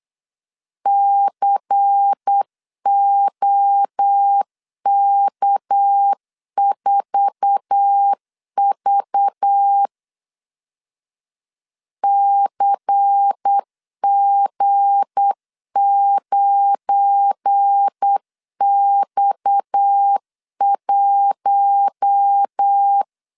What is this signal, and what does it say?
Morse code, heard as a keyed tone, about 8 wpm, 800 Hz. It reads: COK4V CG9X1
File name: LEZIONI / 9 2 8wpm 3437